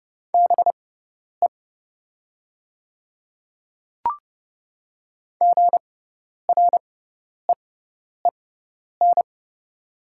Morse code, heard as a keyed tone, about 30 wpm, spaced as for 10 wpm, 690 Hz. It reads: BE GREEN
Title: 6x Character Spacing - Top 500 Words - 30wpm